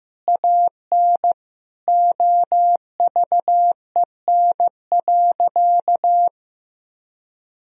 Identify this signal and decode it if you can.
Morse code, heard as a keyed tone, about 15 wpm, 700 Hz. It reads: AN OVEN.
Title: Sentences from Top 1000 Words - Review Part 4 - 15wpm